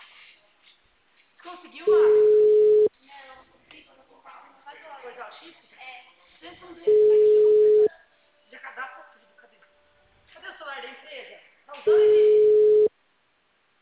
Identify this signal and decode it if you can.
Ringback tone (Europe, 425 Hz), with speech.